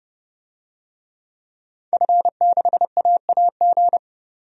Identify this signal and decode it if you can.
Morse code, heard as a keyed tone, about 30 wpm, 700 Hz. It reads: F6AAG
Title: DXCC-100 Call Signs - 5 Characters - 30wpm